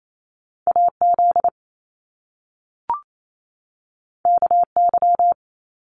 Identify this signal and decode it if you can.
Morse code, heard as a keyed tone, about 28 wpm, 700 Hz. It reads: AZ KY
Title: US States 28wpm